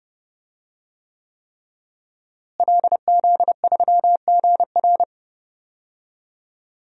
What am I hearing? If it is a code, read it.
Morse code, heard as a keyed tone, about 32 wpm, 700 Hz. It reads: LZ3GR
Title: DXCC-100 Call Signs - 5 Characters - Repeated 3x - 30wpm